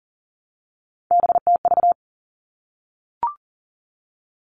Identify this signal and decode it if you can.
Morse code, heard as a keyed tone, about 40 wpm, 700 Hz. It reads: BTV
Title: All Cut Numbers - 3-Digit 40wpm